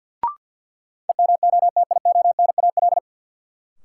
Morse code, heard as a keyed tone, about 51 wpm, 700 Hz. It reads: EMOTIONAL
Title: Sentences from Top 1000 Words - Review Part 5 - New Word Drill - 50wpm